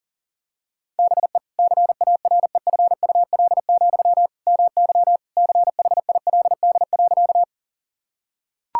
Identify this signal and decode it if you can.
Morse code, heard as a keyed tone, about 40 wpm, 700 Hz. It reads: BE CAREFUL, MY CHILD.
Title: Sentences from Top 1600 Words 40wpm